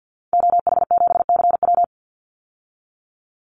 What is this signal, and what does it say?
Morse code, heard as a keyed tone, about 50 wpm, 700 Hz. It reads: OH7CW
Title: Call Signs Repeated 2x - 50wpm